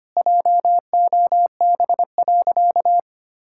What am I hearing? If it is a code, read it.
Morse code, heard as a keyed tone, about 25 wpm, 700 Hz. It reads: JOB.
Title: Sentences from Top 1600 Words 25wpm